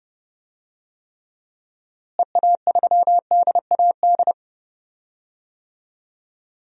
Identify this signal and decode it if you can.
Morse code, heard as a keyed tone, about 30 wpm, 700 Hz. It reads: EA3DAD
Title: DXCC-100 Call Signs - 6 Characters - Repeated 3x - 30wpm